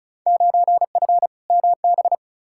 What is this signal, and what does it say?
Morse code, heard as a keyed tone, about 34 wpm, 700 Hz. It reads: 9F MB
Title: Field Day Exchanges Encore - 35wpm